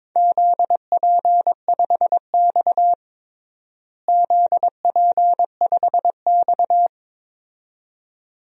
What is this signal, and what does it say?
Morse code, heard as a keyed tone, about 22 wpm, 700 Hz. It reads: ZP5X ZP5X
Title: DXCC-100 Call Signs - 4 Characters - Repeated 2x - 22wpm